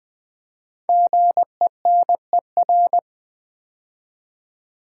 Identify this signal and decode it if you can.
Morse code, heard as a keyed tone, about 20 wpm, 700 Hz. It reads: GENER
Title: Build A Word - 5 Letters At A Time - 10-Letter Words - 20wpm